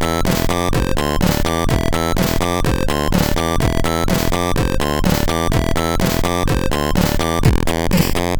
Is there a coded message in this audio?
No.